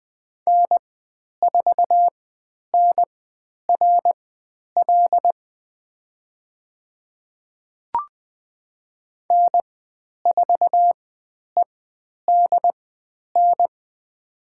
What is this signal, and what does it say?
Morse code, heard as a keyed tone, about 20 wpm, spaced as for 10 wpm, 700 Hz. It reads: N4NRL N4EDN
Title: 095 Lesson 16g - Letter c - Call Signs 20wpm